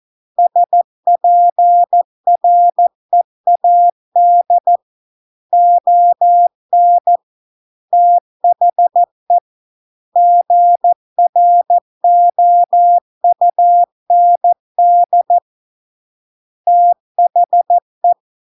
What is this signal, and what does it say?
Morse code, heard as a keyed tone, about 14 wpm, 700 Hz. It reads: SPREAD ON THE GROUND THE